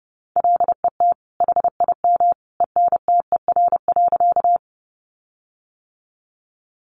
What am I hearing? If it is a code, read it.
Morse code, heard as a keyed tone, about 30 wpm, 700 Hz. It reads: LET HIM ENTER.
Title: Sentences from Top 1300 Words 30wpm